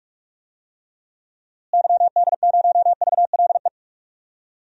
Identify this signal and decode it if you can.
Morse code, heard as a keyed tone, about 46 wpm, 700 Hz. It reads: YD0VLE
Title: DXCC-100 Call Signs - 6 Characters - Repeated 3x - 45wpm